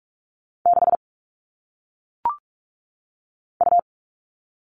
Morse code, heard as a keyed tone, about 46 wpm, 700 Hz. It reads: 6 U
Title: All Cut Numbers - 1-Digit 45wpm